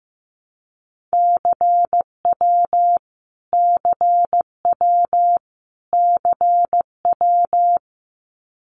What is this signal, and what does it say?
Morse code, heard as a keyed tone, about 15 wpm, 700 Hz. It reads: CW CW CW
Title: Abbreviations Repeated 3x - 15wpm